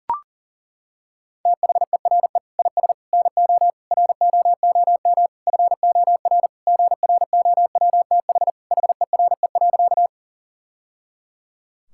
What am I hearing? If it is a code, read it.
Morse code, heard as a keyed tone, about 40 wpm, 700 Hz. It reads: THERE IS NO ROOM FOR GROWTH HERE.
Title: Sentences from Top 1000 Words - Review Part 5 - 40wpm